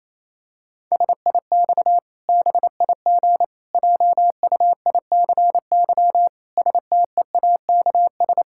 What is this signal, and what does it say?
Morse code, heard as a keyed tone, about 28 wpm, 700 Hz. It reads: SIX BIG JUICY STEAKS
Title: Pangram Challenge v2 - 28wpm